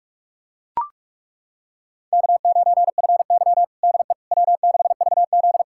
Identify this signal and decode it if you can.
Morse code, heard as a keyed tone, about 42 wpm, 700 Hz. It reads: K9FY DE W6VZ
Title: USA Callsign DE Callsign - 4 Characters - 45wpm